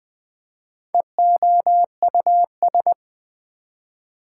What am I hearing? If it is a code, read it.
Morse code, heard as a keyed tone, about 20 wpm, 700 Hz. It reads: EOUS
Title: Prefixes and Suffixes 20wpm